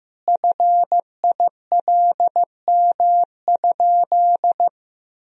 Morse code, heard as a keyed tone, about 16 wpm, 700 Hz. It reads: FILM?
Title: Sentences from Top 1000 Words - Review Part 3 - 15wpm